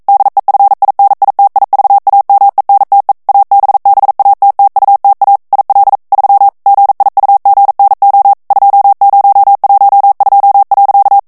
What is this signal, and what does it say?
Morse code, heard as a keyed tone, about 42 wpm, 800 Hz. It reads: DEFINITIVAMENTE ABBATTUTA IL 3 GIUGNO 2012.
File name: ITALIEN texte clair 1 vitesse 40 mots minute